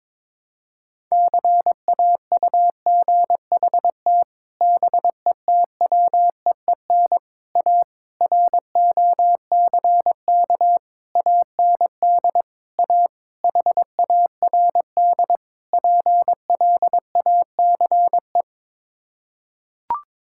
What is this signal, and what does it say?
Morse code, heard as a keyed tone, about 22 wpm, 700 Hz. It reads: CAUGHT BETWEEN A ROCK AND A HARD PLACE
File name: practice / 22wpm / Sets of 8 Words - 22wpm